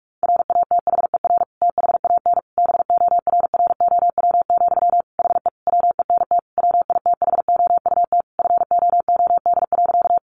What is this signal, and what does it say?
Morse code, heard as a keyed tone, about 44 wpm, 700 Hz. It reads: RATHER THAN BORROW, HE WENT WITHOUT FOOD.
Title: Sentences from Top 1500 Words 45wpm